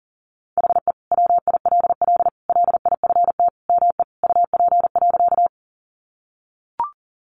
Morse code, heard as a keyed tone, about 40 wpm, 700 Hz. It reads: HE WILL LIFT ME UP.
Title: Sentences from Top 1000 Words 40wpm